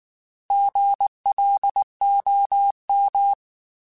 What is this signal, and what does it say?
Morse code, heard as a keyed tone, about 19 wpm, 800 Hz. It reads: GLOM